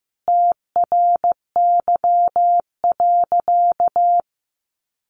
Morse code, heard as a keyed tone, about 15 wpm, 700 Hz. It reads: TRY.
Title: Sentences from Top 1300 Words 15wpm